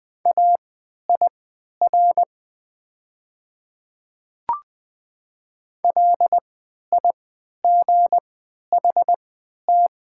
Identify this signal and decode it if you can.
Morse code, heard as a keyed tone, about 20 wpm, spaced as for 11 wpm, 700 Hz. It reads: AIR LIGHT
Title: 3x Character Spacing - Top 300 Words - 20wpm